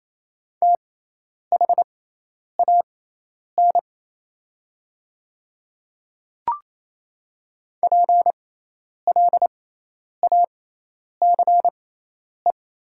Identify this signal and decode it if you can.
Morse code, heard as a keyed tone, about 28 wpm, spaced as for 10 wpm, 700 Hz. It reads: THAN PLACE
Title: 6x Character Spacing - Top 200 Words - 28wpm